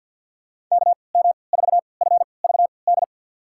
Morse code, heard as a keyed tone, about 51 wpm, spaced as for 28 wpm, 700 Hz. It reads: KM4FVD
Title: USA Call Signs - 6 Characters - 3x Character Spacing - 50wpm